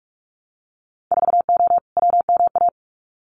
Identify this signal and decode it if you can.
Morse code, heard as a keyed tone, about 46 wpm, 700 Hz. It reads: 4O WMA
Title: Winter Field Day Exchanges Encore 45wpm